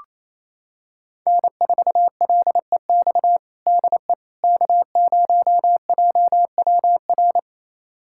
Morse code, heard as a keyed tone, about 28 wpm, 700 Hz. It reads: N4LEX DE K0JWR
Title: USA Callsign DE Callsign - 5 Characters - 28wpm